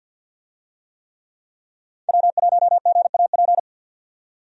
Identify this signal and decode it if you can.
Morse code, heard as a keyed tone, about 52 wpm, 700 Hz. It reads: W1GAP
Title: USA Call Signs - 5 Characters - 50wpm